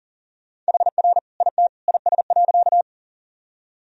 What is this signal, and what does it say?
Morse code, heard as a keyed tone, about 40 wpm, 700 Hz. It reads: HR IT IS.
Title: Sentences from Top 200 Words and Abbreviations 40wpm